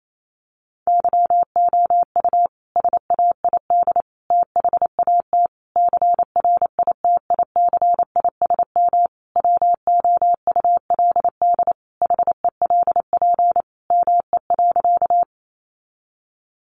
Morse code, heard as a keyed tone, about 28 wpm, 700 Hz. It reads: YOU SAID THAT CRITICISM WOULD HELP ME.
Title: Sentences from Top 1300 Words 28wpm